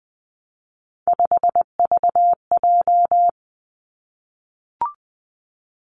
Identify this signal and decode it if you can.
Morse code, heard as a keyed tone, about 20 wpm, 700 Hz. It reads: 5VJ
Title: Three Challenging Letter-Number Combinations 20wpm